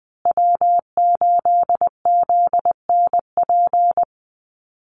Morse code, heard as a keyed tone, about 20 wpm, 700 Hz. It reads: W8ZNP
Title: USA Call Signs Repeated 3x 20wpm